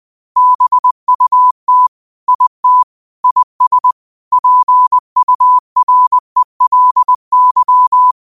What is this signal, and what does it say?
Morse code, heard as a keyed tone, about 20 wpm, 1000 Hz. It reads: BUT IT IS PURELY
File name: qotdfiles / 20wpm